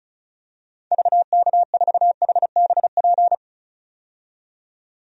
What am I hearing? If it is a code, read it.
Morse code, heard as a keyed tone, about 34 wpm, 700 Hz. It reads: VK4HBP
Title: DXCC-100 Call Signs - 6 Characters - 35wpm